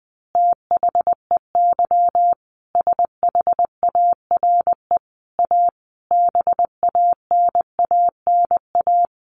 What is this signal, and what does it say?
Morse code, heard as a keyed tone, about 20 wpm, 700 Hz. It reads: THEY SHARE A BANANA
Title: Easy Phrases 20wpm